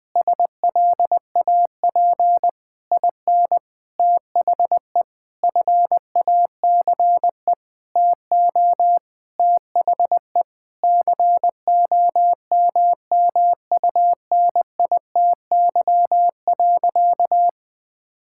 Morse code, heard as a keyed tone, about 20 wpm, 700 Hz. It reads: SLAP IN THE FACE TO THE COMMUNITY.
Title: Sentences with 8-Word Phrases - 20wpm